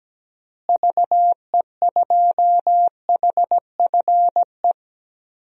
Morse code, heard as a keyed tone, about 17 wpm, 700 Hz. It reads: VE2HFE
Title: DXCC-100 Call Signs - 6 Characters - Repeated 2x - 17wpm